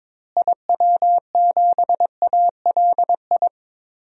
Morse code, heard as a keyed tone, about 22 wpm, 700 Hz. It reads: IW7ALI
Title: DXCC-100 Call Signs - 6 Characters - Repeated 3x - 22wpm